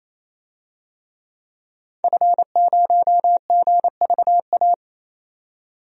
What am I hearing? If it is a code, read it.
Morse code, heard as a keyed tone, about 27 wpm, 700 Hz. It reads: F0GVA
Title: DXCC-100 Call Signs - 5 Characters - 28wpm